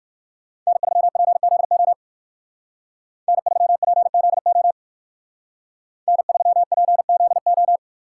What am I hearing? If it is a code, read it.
Morse code, heard as a keyed tone, about 52 wpm, 700 Hz. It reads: N3PZO N3PZO N3PZO
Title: Speed Racing - USA Call Signs 45wpm